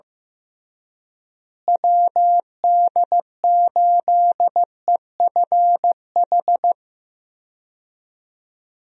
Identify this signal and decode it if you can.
Morse code, heard as a keyed tone, about 15 wpm, 700 Hz. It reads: WD8EFH
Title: USA Call Signs - 6 Characters - Repeated 3x - 15wpm